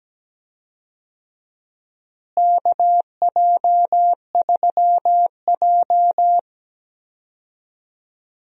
Morse code, heard as a keyed tone, about 17 wpm, 700 Hz. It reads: KJ3J
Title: USA Call Signs - 4 Characters - Repeated 3x - 17wpm